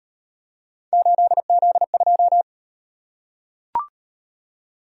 Morse code, heard as a keyed tone, about 38 wpm, 700 Hz. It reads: 8Z2